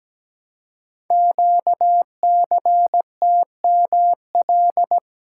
Morse code, heard as a keyed tone, about 17 wpm, 700 Hz. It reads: QCTML
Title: Code Groups - 5 Letters - 17wpm